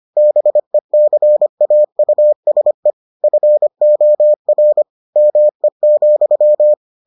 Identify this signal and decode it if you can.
Morse code, heard as a keyed tone, about 25 wpm, 600 Hz. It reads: BECAUSE FOR ME,